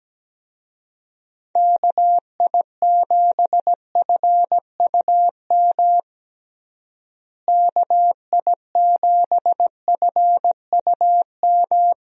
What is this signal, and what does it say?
Morse code, heard as a keyed tone, about 17 wpm, 700 Hz. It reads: KI7FUM KI7FUM
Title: USA Call Signs - 6 Characters - Repeated 2x - 17wpm